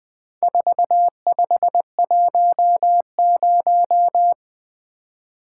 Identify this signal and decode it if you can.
Morse code, heard as a keyed tone, about 20 wpm, 700 Hz. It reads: 4510
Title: Four Letter-Number Combinations - 20wpm